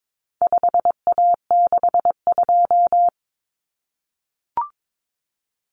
Morse code, heard as a keyed tone, about 22 wpm, 700 Hz. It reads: 5A62
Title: Common Cut Numbers - 4-Digit 22wpm